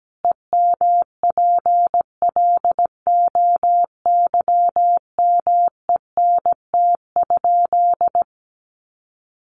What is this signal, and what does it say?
Morse code, heard as a keyed tone, about 17 wpm, 700 Hz. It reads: EMPLOYMENT?
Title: Sentences from Top 1500 Words 17wpm